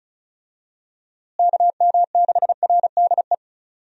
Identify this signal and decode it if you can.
Morse code, heard as a keyed tone, about 36 wpm, 700 Hz. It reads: KM6RDE
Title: USA Call Signs - 6 Characters - Repeated 2x - 35wpm